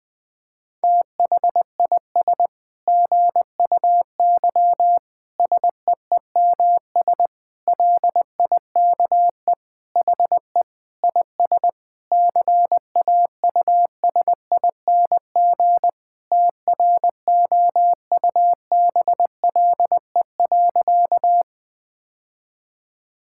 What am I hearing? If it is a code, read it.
Morse code, heard as a keyed tone, about 20 wpm, 700 Hz. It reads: THIS GUY SEEMS LIKE HE IS CAUSING TROUBLE.
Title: Sentences with 9-Word Phrases - 20wpm